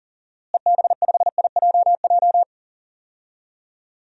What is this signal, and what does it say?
Morse code, heard as a keyed tone, about 40 wpm, 700 Hz. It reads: EB5IJJ
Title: DXCC-100 Call Signs - 6 Characters - Repeated 2x - 40wpm